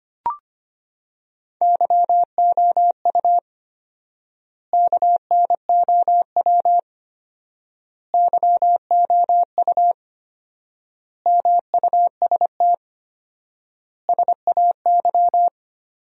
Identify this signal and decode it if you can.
Morse code, heard as a keyed tone, about 25 wpm, 700 Hz. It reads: YOU KNOW YOU MUST SAY
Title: 4x Word-Spacing - Sentences from Top 500 Words - 25wpm